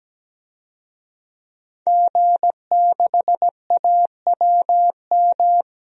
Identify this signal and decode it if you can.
Morse code, heard as a keyed tone, about 17 wpm, 700 Hz. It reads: G6AWM
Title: DXCC-100 Call Signs - 5 Characters - 17wpm